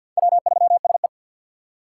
Morse code, heard as a keyed tone, about 47 wpm, 700 Hz. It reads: W3SE
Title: USA Call Signs - 4x Character Spacing - 50wpm